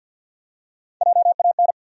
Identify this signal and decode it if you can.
Morse code, heard as a keyed tone, about 52 wpm, 700 Hz. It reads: JAN
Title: USA Common Names - Max 4 Letters - 50wpm